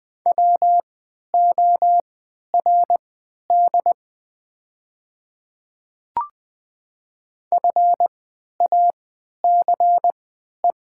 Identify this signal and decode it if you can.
Morse code, heard as a keyed tone, about 20 wpm, spaced as for 11 wpm, 700 Hz. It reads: WORD FACE
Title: 3x Character Spacing - Top 300 Words - 20wpm